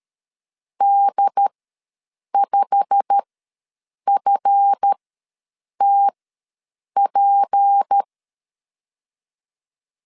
Morse code, heard as a keyed tone, about 13 wpm, spaced as for 7 wpm, 800 Hz. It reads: D5FTP
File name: LEZIONI / 8 YJ45 / 8 YJ45 ws 12wpm 3146